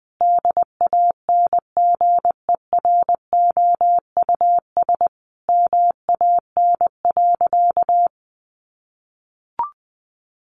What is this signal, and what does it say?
Morse code, heard as a keyed tone, about 20 wpm, 700 Hz. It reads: DANGEROUS MAN.
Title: Sentences from Top 1200 Words 20wpm